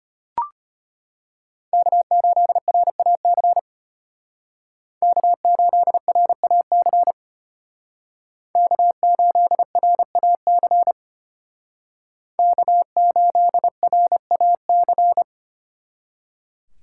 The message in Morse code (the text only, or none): K8RAC K8RAC K8RAC K8RAC